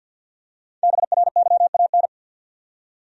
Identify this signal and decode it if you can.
Morse code, heard as a keyed tone, about 52 wpm, 700 Hz. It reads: BRYAN
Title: USA Common Names - Max 5 Letters - 50wpm